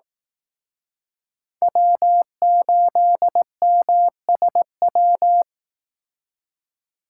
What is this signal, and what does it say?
Morse code, heard as a keyed tone, about 18 wpm, 700 Hz. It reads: W8MSW
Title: Speed Racing - USA Call Signs 15wpm